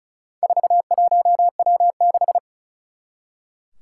Morse code, heard as a keyed tone, about 36 wpm, 700 Hz. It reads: 41W6
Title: Four Challenging Letter-Number Combinations 35wpm